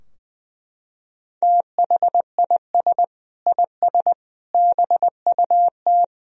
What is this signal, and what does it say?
Morse code, heard as a keyed tone, about 20 wpm, 700 Hz. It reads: THIS IS BUT
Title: Sentences from Top 1000 Words - Review Part 2 - 20wpm